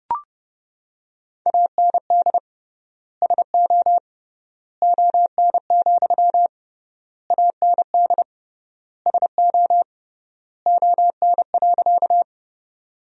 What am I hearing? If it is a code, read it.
Morse code, heard as a keyed tone, about 31 wpm, 700 Hz. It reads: AND SO ON, AND SO ON.
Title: 3x Word-Spacing - Sentences from Top 100 Words - 30wpm